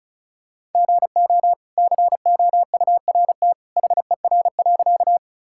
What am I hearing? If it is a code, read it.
Morse code, heard as a keyed tone, about 36 wpm, 700 Hz. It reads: GO COURT HER.
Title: Sentences from Top 1600 Words 35wpm